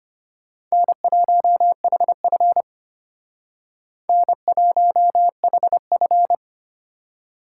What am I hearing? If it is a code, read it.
Morse code, heard as a keyed tone, about 33 wpm, 700 Hz. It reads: N1HF N1HF